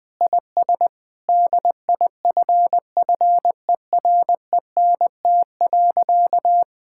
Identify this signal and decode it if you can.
Morse code, heard as a keyed tone, about 20 wpm, 700 Hz. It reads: IS DIFFERENT.
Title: Sentences with 4-Word Phrases - Spoken Beginning - 20wpm